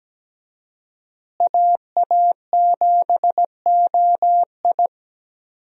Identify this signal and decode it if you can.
Morse code, heard as a keyed tone, about 17 wpm, 700 Hz. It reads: AA7OI
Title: USA Call Signs - 5 Characters - Repeated 3x - 17wpm